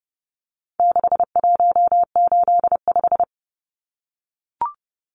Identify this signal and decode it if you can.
Morse code, heard as a keyed tone, about 30 wpm, 700 Hz. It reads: 6185
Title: Numbers - 4-Digit 30wpm